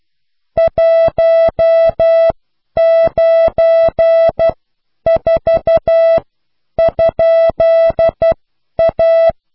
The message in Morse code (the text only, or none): 194?A